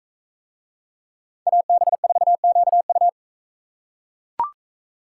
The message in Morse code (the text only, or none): AB4QU